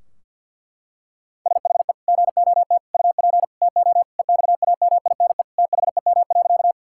Morse code, heard as a keyed tone, about 51 wpm, 700 Hz. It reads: SHE GOT UP TO EXAMINE THEM.
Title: Sentences from Top 1600 Words 50wpm